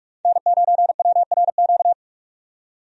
Morse code, heard as a keyed tone, about 46 wpm, 700 Hz. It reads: N9WRQ
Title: USA Call Signs - 5 Characters - 45wpm